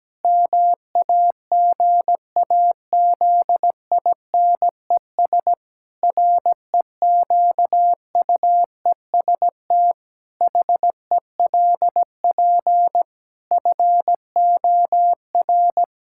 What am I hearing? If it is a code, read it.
Morse code, heard as a keyed tone, about 17 wpm, 700 Hz. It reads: MAGAZINES REQUEST HELP FOR
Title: Pangram Challenge v2 - 17wpm